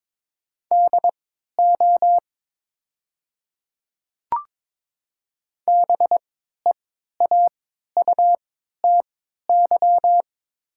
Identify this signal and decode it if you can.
Morse code, heard as a keyed tone, about 22 wpm, spaced as for 12 wpm, 700 Hz. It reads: DO BEAUTY
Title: 3x Character Spacing - Top 500 Words - 22wpm